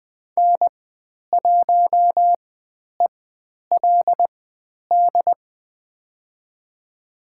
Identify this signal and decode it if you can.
Morse code, heard as a keyed tone, about 20 wpm, spaced as for 10 wpm, 700 Hz. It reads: N1ELD
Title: 095 Lesson 16g - Letter c - Call Signs 20wpm